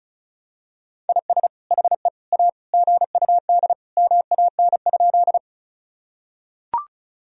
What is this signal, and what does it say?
Morse code, heard as a keyed tone, about 36 wpm, 700 Hz. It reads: IS HE A GUD MAN?
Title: Sentences from Top 200 Words and Abbreviations 35wpm